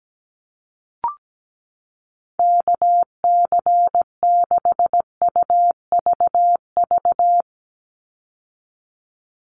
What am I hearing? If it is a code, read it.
Morse code, heard as a keyed tone, about 17 wpm, 700 Hz. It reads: KC6UVV
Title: USA Call Signs 17wpm